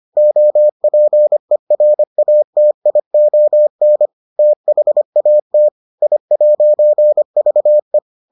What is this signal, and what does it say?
Morse code, heard as a keyed tone, about 25 wpm, 600 Hz. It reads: OPERATION THAT I'VE